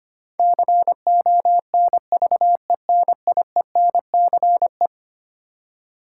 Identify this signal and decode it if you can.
Morse code, heard as a keyed tone, about 25 wpm, 700 Hz. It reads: CONVENIENCE
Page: morse-code-ninja-practice-25wpm